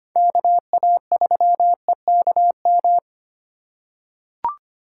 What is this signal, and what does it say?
Morse code, heard as a keyed tone, about 25 wpm, 700 Hz. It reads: KA3EKM
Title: USA Call Signs - 4x Character Spacing - 25wpm